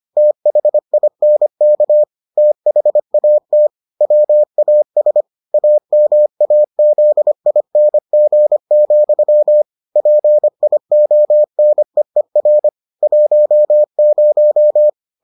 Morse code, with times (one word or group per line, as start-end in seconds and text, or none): 0.2-2.0 THINK
2.4-3.7 THAT
4.0-5.2 WAS
5.5-9.6 AMAZING,
10.0-12.7 PIONEER
13.0-14.9 10